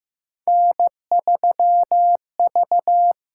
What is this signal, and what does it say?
Morse code, heard as a keyed tone, about 15 wpm, 700 Hz. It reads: N3V
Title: USA Call Signs - 5 Characters - 6x Character Spacing - 15wpm